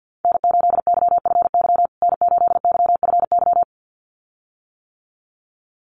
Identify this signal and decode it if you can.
Morse code, heard as a keyed tone, about 50 wpm, 700 Hz. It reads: N8YFY N8YFY
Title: USA Call Signs Repeated 2x 50wpm